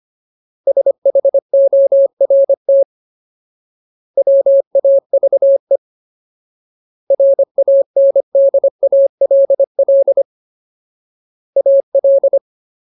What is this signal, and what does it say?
Morse code, heard as a keyed tone, about 25 wpm, 550 Hz. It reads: SHORT WAVE RANDALL AL
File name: Ultimate Word List / 25WPM Ultimate